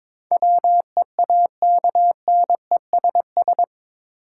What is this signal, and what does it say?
Morse code, heard as a keyed tone, about 22 wpm, 700 Hz. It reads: WEAKNESS
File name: practice / 22wpm / Eight Letter Words - 22wpm